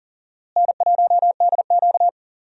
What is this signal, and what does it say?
Morse code, heard as a keyed tone, about 40 wpm, 700 Hz. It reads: N1DQ
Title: USA Call Signs - 4 Characters - Repeated 2x - 40wpm